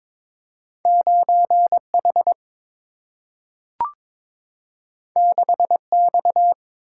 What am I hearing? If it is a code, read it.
Morse code, heard as a keyed tone, about 22 wpm, 700 Hz. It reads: 9 H 6 X
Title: Single-Letter Single-Number 22wpm